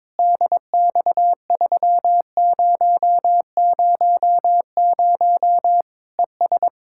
Morse code, heard as a keyed tone, about 22 wpm, 700 Hz. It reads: DX3000 ES
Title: K1USN SST QSO Practice 22wpm